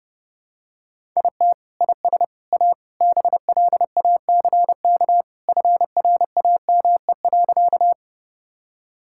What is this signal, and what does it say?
Morse code, heard as a keyed tone, about 30 wpm, 700 Hz. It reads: IT IS A BLACK FRAME.
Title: Sentences from Top 1000 Words - Review Part 6 - 30wpm